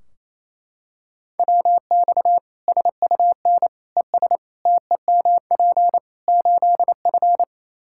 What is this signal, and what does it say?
Morse code, heard as a keyed tone, about 28 wpm, 700 Hz. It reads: WX SUN ES TEMP 8F
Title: Practice QSO - Exchange 2 - Info - 28wpm